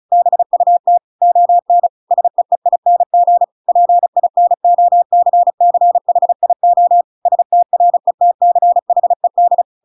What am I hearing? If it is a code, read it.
Morse code, heard as a keyed tone, about 34 wpm, 700 Hz. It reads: BUT ON SEEING PINOCCHIO STRETCHED
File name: Pinocchio / Pinocchio ch7 35 35 ik1hge